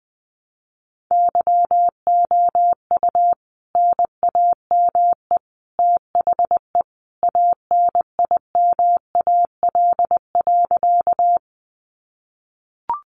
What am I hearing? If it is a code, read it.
Morse code, heard as a keyed tone, about 19 wpm, 700 Hz. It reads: YOU NAME THE ANIMAL.
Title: A Sound Of Thunder 20wpm